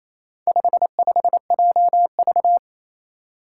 Morse code, heard as a keyed tone, about 28 wpm, 700 Hz. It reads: 55JV
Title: Four Challenging Letter-Number Combinations 28wpm